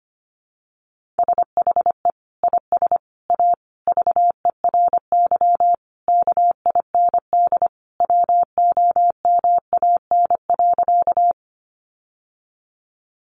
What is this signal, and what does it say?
Morse code, heard as a keyed tone, about 25 wpm, 700 Hz. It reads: SHE IS A VERY KIND WOMAN.
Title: Sentences from Top 1100 Words 25wpm